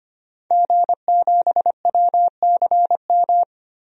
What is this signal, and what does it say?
Morse code, heard as a keyed tone, about 25 wpm, 700 Hz. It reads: G7WCM
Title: DXCC-100 Call Signs - 5 Characters - 25wpm